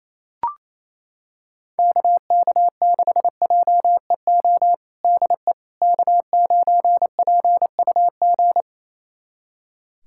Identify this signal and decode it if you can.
Morse code, heard as a keyed tone, about 28 wpm, 700 Hz. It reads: KK6JEO DE K9PUG
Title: USA Callsign DE Callsign 28wpm